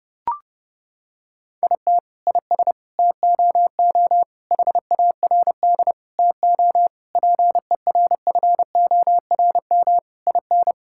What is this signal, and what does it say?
Morse code, heard as a keyed tone, about 32 wpm, 700 Hz. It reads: IT IS TOO HARD TO PERFORM IN
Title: Sentences from Top 1000 Words - Review Part 5 - 30wpm